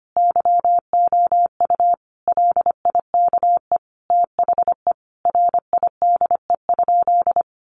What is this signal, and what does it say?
Morse code, heard as a keyed tone, about 25 wpm, 700 Hz. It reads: YOU LIKE THE RIDE?
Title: Sentences from Top 1200 Words 25wpm